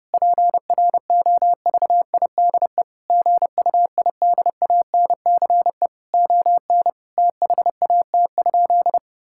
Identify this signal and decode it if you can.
Morse code, heard as a keyed tone, about 30 wpm, 700 Hz. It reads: PROVIDE GUIDANCE ON THAT?
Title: Sentences from Top 1600 Words 30wpm